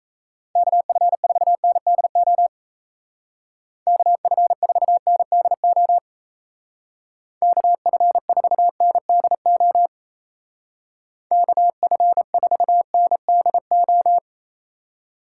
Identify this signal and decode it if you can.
Morse code, heard as a keyed tone, about 35 wpm, 700 Hz. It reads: KF4NDO KF4NDO KF4NDO KF4NDO
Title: Speed Racing - USA 2x3 Call Signs 28wpm